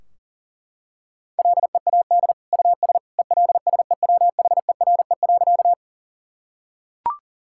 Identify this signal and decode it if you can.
Morse code, heard as a keyed tone, about 40 wpm, 700 Hz. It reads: LEAD US ELSEWHERE.
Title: Sentences from Top 1600 Words 40wpm